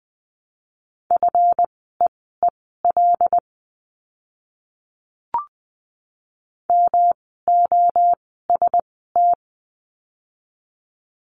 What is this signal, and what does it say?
Morse code, heard as a keyed tone, about 20 wpm, spaced as for 14 wpm, 700 Hz. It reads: FEEL MOST
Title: 139 Lesson 23f - Letter f - Words 20wpm